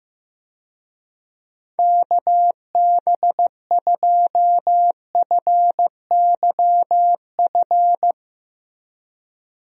Morse code, heard as a keyed tone, about 16 wpm, 700 Hz. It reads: KB2FYF
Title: USA Call Signs - 6 Characters - Repeated 3x - 15wpm